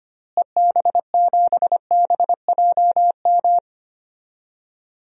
Morse code, heard as a keyed tone, about 25 wpm, 700 Hz. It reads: EB7BJM
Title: DXCC-100 Call Signs - 6 Characters - Repeated 3x - 25wpm